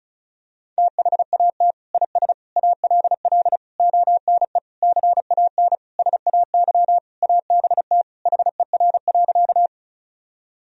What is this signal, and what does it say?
Morse code, heard as a keyed tone, about 36 wpm, 700 Hz. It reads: THAT IS ALL ONE CAN SAY ABT HER.
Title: Sentences from Top 200 Words and Abbreviations 35wpm